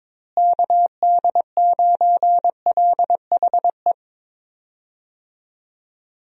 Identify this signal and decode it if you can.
Morse code, heard as a keyed tone, about 22 wpm, 700 Hz. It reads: KD9LHE
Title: USA Call Signs - 6 Characters - Repeated 2x - 22wpm